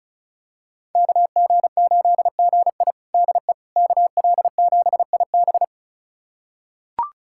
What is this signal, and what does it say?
Morse code, heard as a keyed tone, about 36 wpm, 700 Hz. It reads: KG8GI DE KL7IB
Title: USA Callsign DE Callsign 35wpm